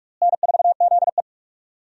Morse code, heard as a keyed tone, about 42 wpm, 700 Hz. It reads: N4ZE
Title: USA Call Signs - 4 Characters - Repeated 3x - 45wpm